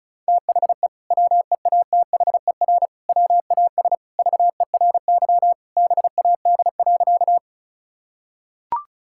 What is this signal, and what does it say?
Morse code, heard as a keyed tone, about 36 wpm, 700 Hz. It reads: THE WEATHER WAS VERY BAD.
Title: Sentences from Top 1000 Words - Review Part 5 - 35wpm